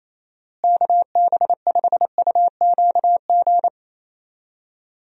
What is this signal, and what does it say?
Morse code, heard as a keyed tone, about 28 wpm, 700 Hz. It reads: KB5UQG
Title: USA Call Signs - 6 Characters - Repeated 2x - 28wpm